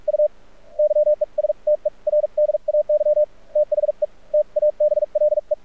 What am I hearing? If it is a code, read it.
Morse code, heard as a keyed tone, about 46 wpm, spaced as for 38 wpm, 600 Hz. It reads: U YESTERDAY THE TABLE
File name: archived / CRSnet